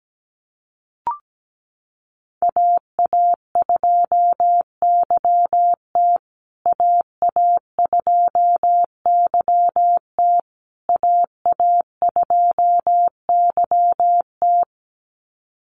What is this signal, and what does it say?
Morse code, heard as a keyed tone, about 17 wpm, 700 Hz. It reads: AA2YT AA2YT AA2YT
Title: USA Call Signs Repeated 3x 17wpm